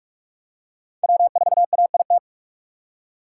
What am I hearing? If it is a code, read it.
Morse code, heard as a keyed tone, about 39 wpm, 700 Hz. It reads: W4AIT